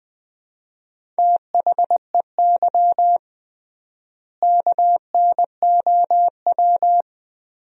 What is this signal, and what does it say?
Morse code, heard as a keyed tone, about 20 wpm, 700 Hz. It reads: THEY KNOW
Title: 3x Word-Spacing - Sentences from Top 500 Words - 20wpm